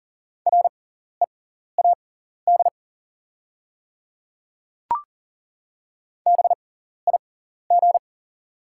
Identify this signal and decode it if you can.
Morse code, heard as a keyed tone, about 40 wpm, spaced as for 14 wpm, 700 Hz. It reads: READ BIG
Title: 6x Character Spacing - Top 200 Words - 40wpm